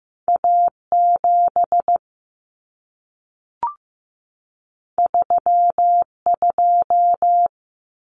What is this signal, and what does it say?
Morse code, heard as a keyed tone, about 15 wpm, 700 Hz. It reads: A7 32
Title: Common Cut Numbers - 2-Digit 15wpm